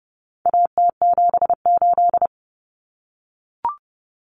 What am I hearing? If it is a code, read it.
Morse code, heard as a keyed tone, about 30 wpm, 700 Hz. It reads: AT78
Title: Common Cut Numbers - 4-Digit 30wpm